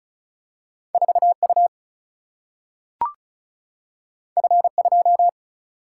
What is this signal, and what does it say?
Morse code, heard as a keyed tone, about 36 wpm, 700 Hz. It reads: 4U F2
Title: Single-Letter Single-Number 35wpm